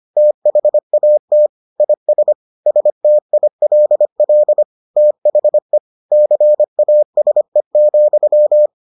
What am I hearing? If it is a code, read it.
Morse code, heard as a keyed tone, about 25 wpm, 600 Hz. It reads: THAT IS STILL THE CASE,